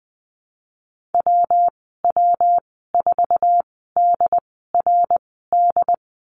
Morse code, heard as a keyed tone, about 20 wpm, spaced as for 15 wpm, 700 Hz. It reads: WW4DRD
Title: 167 Lesson 27g - Number 7 - Call Signs 20wpm